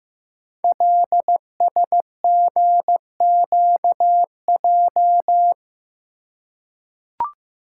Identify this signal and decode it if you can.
Morse code, heard as a keyed tone, about 15 wpm, 700 Hz. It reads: LSGQJ
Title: Code Groups - 5 Letter:Numbers - 15wpm